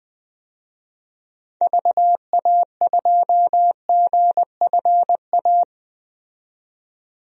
Morse code, heard as a keyed tone, about 20 wpm, 700 Hz. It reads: VA2GFA
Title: DXCC-100 Call Signs - 6 Characters - 20wpm